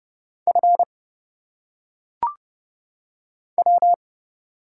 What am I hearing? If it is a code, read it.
Morse code, heard as a keyed tone, about 31 wpm, 700 Hz. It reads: F W
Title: Single Letters - Phonetic Pronunciation 30wpm